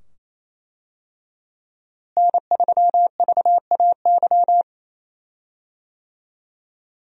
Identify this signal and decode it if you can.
Morse code, heard as a keyed tone, about 28 wpm, 700 Hz. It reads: N3VAY
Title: USA Call Signs - 5 Characters - Repeated 2x - 28wpm